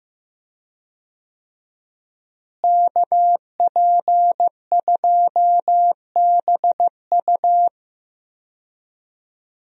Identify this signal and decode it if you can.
Morse code, heard as a keyed tone, about 15 wpm, 700 Hz. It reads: KP2BU
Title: USA Call Signs - 5 Characters - Repeated 3x - 15wpm